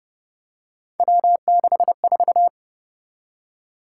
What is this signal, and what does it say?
Morse code, heard as a keyed tone, about 30 wpm, 700 Hz. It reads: W64
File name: Speed-Racing - Three Letter-Number Combinations - 25wpm